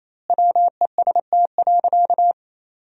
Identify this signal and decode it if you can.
Morse code, heard as a keyed tone, about 28 wpm, 700 Hz. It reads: WEST.
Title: 3x Word-Spacing - Sentences from Top 400 Words - 28wpm